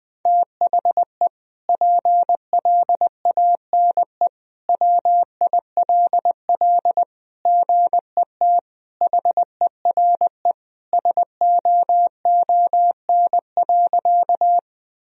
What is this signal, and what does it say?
Morse code, heard as a keyed tone, about 20 wpm, 700 Hz. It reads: THE PLANE WILL GET HERE SOON.
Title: Sentences from Top 1000 Words - Review Part 3 - 20wpm